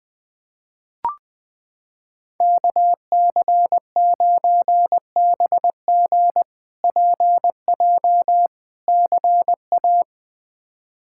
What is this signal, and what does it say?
Morse code, heard as a keyed tone, about 20 wpm, 700 Hz. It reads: KC9BG PJ CA
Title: K1USN SST QSO Practice - 20wpm